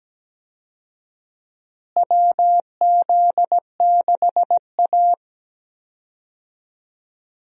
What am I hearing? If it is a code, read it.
Morse code, heard as a keyed tone, about 17 wpm, 700 Hz. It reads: WZ6A
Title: USA Call Signs - 4 Characters - Repeated 2x - 17wpm